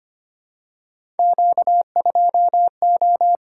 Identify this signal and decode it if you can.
Morse code, heard as a keyed tone, about 25 wpm, 700 Hz. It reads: Q2O
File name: Three Letter-Number Combinations - 25wpm